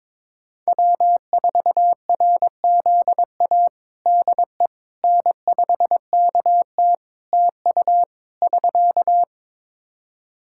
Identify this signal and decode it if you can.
Morse code, heard as a keyed tone, about 21 wpm, 700 Hz. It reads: W4RZA DE N5KT TU <SK>
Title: Practice QSO - Exchange 4 - Ending - 22wpm